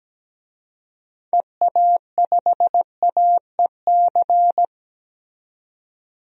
Morse code, heard as a keyed tone, about 17 wpm, 700 Hz. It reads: EA5AEC
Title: DXCC-100 Call Signs - 6 Characters - 17wpm